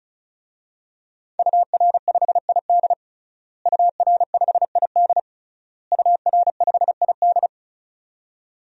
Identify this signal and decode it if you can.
Morse code, heard as a keyed tone, about 36 wpm, 700 Hz. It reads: UR5ID UR5ID UR5ID
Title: DXCC-100 Call Signs - 5 Characters - Repeated 3x - 35wpm